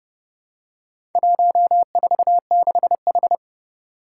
Morse code, heard as a keyed tone, about 30 wpm, 700 Hz. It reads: 146H
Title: Four Challenging Letter-Number Combinations 30wpm